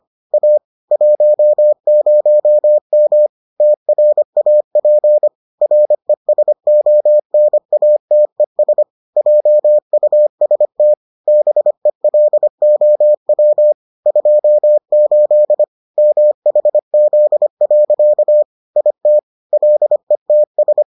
Morse code, heard as a keyed tone, about 25 wpm, 600 Hz. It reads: A 10M TRAP RESONATES JUST BELOW 28 MHZ. IT LETS